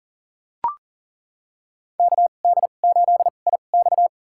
Morse code, heard as a keyed tone, about 40 wpm, spaced as for 29 wpm, 700 Hz. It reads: KD8IX
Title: USA Call Signs - 5 Characters - 2x Character Spacing - 40wpm